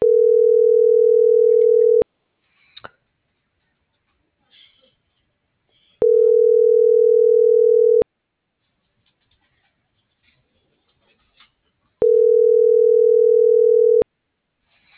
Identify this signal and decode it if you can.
Ringback tone (North America), 440 + 480 Hz.